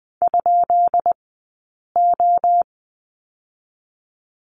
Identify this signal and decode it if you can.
Morse code, heard as a keyed tone, about 20 wpm, spaced as for 8 wpm, 700 Hz. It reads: ?O
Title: 183 Lesson 30d - Letter b - Two Characters 20wpm